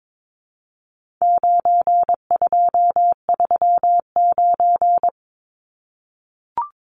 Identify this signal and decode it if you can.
Morse code, heard as a keyed tone, about 22 wpm, 700 Hz. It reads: 9239
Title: Numbers - 4-Digit 22wpm